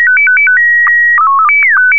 SSTV (slow-scan television), Scottie 1.
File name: ham / sstv test-1